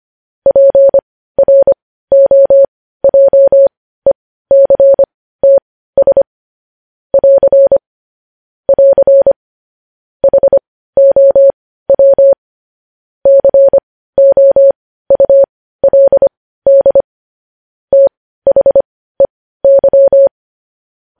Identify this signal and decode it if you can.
Morse code, heard as a keyed tone, about 25 wpm, spaced as for 15 wpm, 550 Hz. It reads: PROJECTS <AR> <AR> HOW COULD THEY
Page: news headlines in morse code at 15 wpm